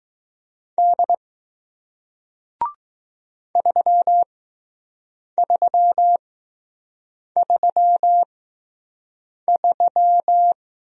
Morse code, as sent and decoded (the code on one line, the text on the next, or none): -.. / ...-- ...-- ...-- ...--
D 3333